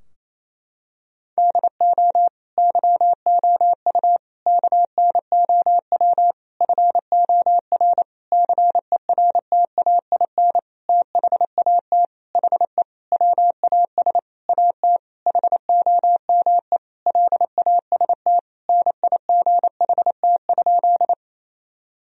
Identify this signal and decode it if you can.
Morse code, heard as a keyed tone, about 28 wpm, 700 Hz. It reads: DO YOU KNOW FOR CERTAIN THAT HE WAS AT HOME LAST NIGHT?
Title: Sentences from Top 1000 Words - Review Part 2 - 28wpm